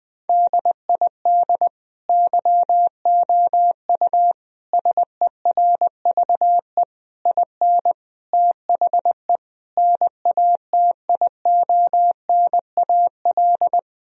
Morse code, heard as a keyed tone, about 20 wpm, 700 Hz. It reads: DID YOU SERVE IN THE NATIONAL
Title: Sentences from Top 1000 Words - Review Part 4 - 20wpm